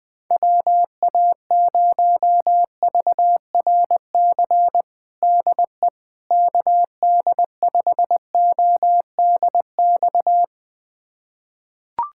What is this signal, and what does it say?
Morse code, heard as a keyed tone, about 20 wpm, 700 Hz. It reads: WA0VRC DE KD5ODX
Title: USA Callsign DE Callsign - 6 Characters - 20wpm